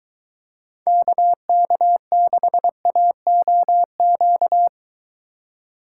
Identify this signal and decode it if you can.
Morse code, heard as a keyed tone, about 23 wpm, 700 Hz. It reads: KK6AOQ